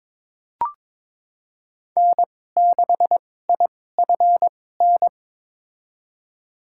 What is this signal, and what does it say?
Morse code, heard as a keyed tone, about 22 wpm, spaced as for 16 wpm, 700 Hz. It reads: N6IFN
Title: USA Call Signs - 2x Character Spacing - 22wpm